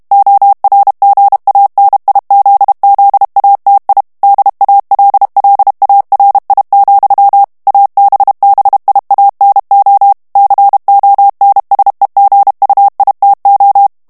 Morse code, heard as a keyed tone, about 32 wpm, 800 Hz. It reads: ORGANIZZATI DALLARI, ABBIANO CONSEGUITO
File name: texte clair 2 vitesse 30 mots minute